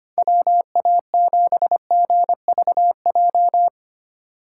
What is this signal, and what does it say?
Morse code, heard as a keyed tone, about 25 wpm, 700 Hz. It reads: WA7GVJ
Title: USA Call Signs - 6 Characters - 4x Character Spacing - 25wpm